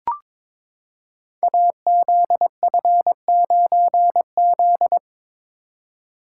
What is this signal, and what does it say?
Morse code, heard as a keyed tone, about 22 wpm, 700 Hz. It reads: AZF9Z
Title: Code Groups - 5 Letter:Numbers - 22wpm